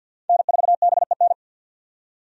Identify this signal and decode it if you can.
Morse code, heard as a keyed tone, about 51 wpm, 700 Hz. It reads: N4BEN
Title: USA Call Signs - 5 Characters - 50wpm